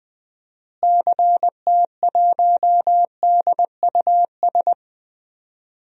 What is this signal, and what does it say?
Morse code, heard as a keyed tone, about 20 wpm, 700 Hz. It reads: CT1DUS
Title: DXCC-100 Call Signs - 6 Characters - Repeated 2x - 20wpm